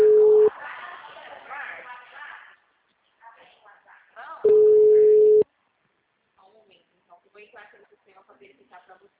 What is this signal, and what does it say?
Ringback tone (Europe, 425 Hz), with speech.
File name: out- -2005- - - .1249